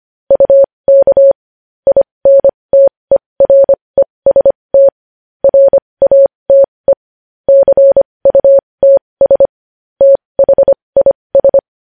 Morse code, heard as a keyed tone, about 25 wpm, spaced as for 20 wpm, 550 Hz. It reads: UK INTEREST RATE CUTS THIS
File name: podcast / News headlines-1x-25x20